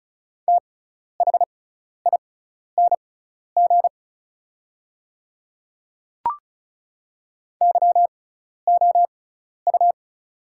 Morse code, heard as a keyed tone, about 36 wpm, spaced as for 12 wpm, 690 Hz. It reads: THING YOU
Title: 6x Character Spacing - Top 100 Words - 35wpm